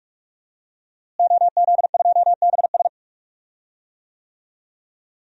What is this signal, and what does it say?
Morse code, heard as a keyed tone, about 46 wpm, 700 Hz. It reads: OZ2BS
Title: DXCC-100 Call Signs - 5 Characters - Repeated 2x - 45wpm